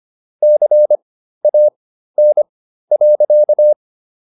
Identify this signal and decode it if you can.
Morse code, heard as a keyed tone, about 25 wpm, spaced as for 13 wpm, 600 Hz. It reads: CAN.